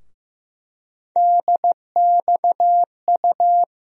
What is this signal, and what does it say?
Morse code, heard as a keyed tone, about 15 wpm, 700 Hz. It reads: DXU
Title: USA Call Sign Postfixes - 3 Letters 15wpm